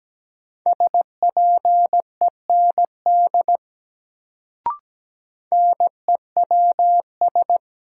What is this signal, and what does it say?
Morse code, heard as a keyed tone, about 17 wpm, 700 Hz. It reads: SPEND NEWS
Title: Sentences from Top 800 Words - New Word Drill - 17wpm